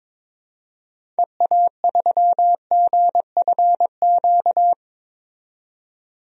Morse code, heard as a keyed tone, about 22 wpm, 700 Hz. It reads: EA3GFQ
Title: DXCC-100 Call Signs - 6 Characters - 22wpm